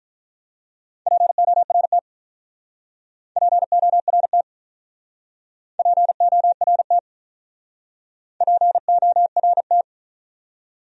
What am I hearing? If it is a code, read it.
Morse code, heard as a keyed tone, about 42 wpm, 700 Hz. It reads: PORT PORT PORT PORT
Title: Speed-Racing Top 200 Words 35wpm